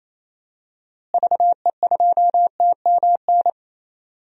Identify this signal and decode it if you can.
Morse code, heard as a keyed tone, about 28 wpm, 700 Hz. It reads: VE2TMN
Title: DXCC-100 Call Signs - 6 Characters - Repeated 2x - 28wpm